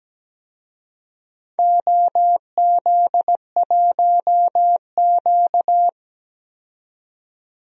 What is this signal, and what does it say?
Morse code, heard as a keyed tone, about 17 wpm, 700 Hz. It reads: OZ1Q
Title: DXCC-100 Call Signs - 4 Characters - 17wpm